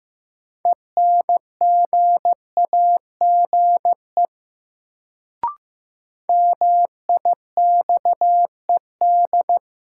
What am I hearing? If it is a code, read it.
Morse code, heard as a keyed tone, about 15 wpm, 700 Hz. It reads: ENGAGE MIXED
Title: Sentences from Top 1500 Words - New Word Drill - 15wpm